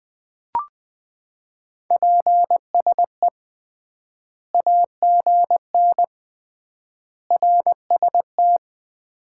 Morse code, heard as a keyed tone, about 20 wpm, 700 Hz. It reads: PSE AGN RST
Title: Common QSO Phrases - 3x Word-Spacing - 20wpm